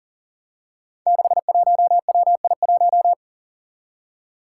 Morse code, heard as a keyed tone, about 39 wpm, 700 Hz. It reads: 61WI1
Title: Five Challenging Letter-Number Combinations 40wpm